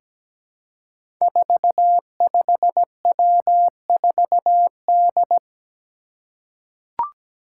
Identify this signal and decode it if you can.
Morse code, heard as a keyed tone, about 17 wpm, 700 Hz. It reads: 45W4D
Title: Five Challenging Letter-Number Combinations 17wpm